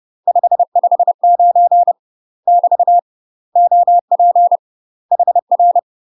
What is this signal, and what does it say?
Morse code, heard as a keyed tone, about 30 wpm, 700 Hz. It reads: 559 <BT> OP HR
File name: UK MASTER QSOs TRAINING / 30WPM / UK MASTER QSOs 0044